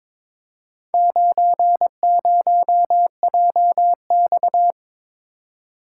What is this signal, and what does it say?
Morse code, heard as a keyed tone, about 22 wpm, 700 Hz. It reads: 90JX
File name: practice / 22wpm / Four Letter-Number Combinations - 22wpm